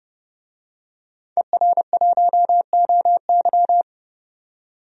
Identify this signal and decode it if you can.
Morse code, heard as a keyed tone, about 30 wpm, 700 Hz. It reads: ER1OY
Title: DXCC-100 Call Signs - 5 Characters - 30wpm